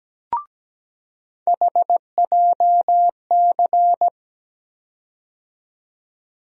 Morse code, heard as a keyed tone, about 17 wpm, 700 Hz. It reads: HJC
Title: USA Call Sign Postfixes - 3 Letters 17wpm